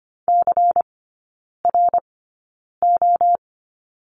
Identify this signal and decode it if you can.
Morse code, heard as a keyed tone, about 25 wpm, spaced as for 9 wpm, 700 Hz. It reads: CRO
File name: course / 25:10wpm / 093 Lesson 16e - Letter c - Three Characters 25wpm